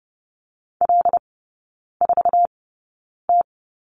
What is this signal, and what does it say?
Morse code, heard as a keyed tone, about 30 wpm, spaced as for 9 wpm, 690 Hz. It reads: L4T
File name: Lesson 17e - Letter u - Three Characters 30wpm